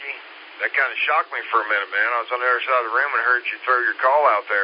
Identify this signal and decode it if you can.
Emergency Alert System (SAME) tones, heard as harsh buzzing bursts, with speech. It is heard from 0.5 to 3.4 s.